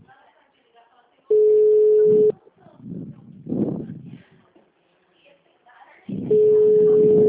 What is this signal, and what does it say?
Ringback tone (Europe, 425 Hz), with speech.
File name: out- -2005- - - .34124